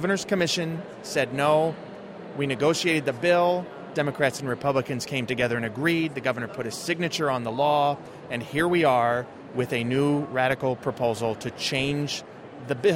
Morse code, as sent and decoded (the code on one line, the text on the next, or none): none